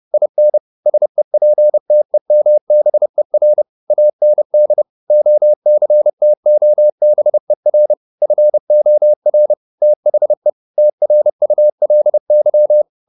Morse code, heard as a keyed tone, about 30 wpm, 600 Hz. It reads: IN SEPTEMBER AND OCTOBER FOR THE TRULY
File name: the old man and the sea / the old man and the sea 1 30wpm